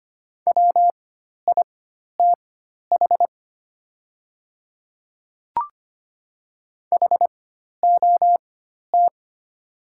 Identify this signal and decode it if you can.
Morse code, heard as a keyed tone, about 25 wpm, spaced as for 12 wpm, 700 Hz. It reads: WITH HOT
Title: 4x Character Spacing - Top 100 Words - 25wpm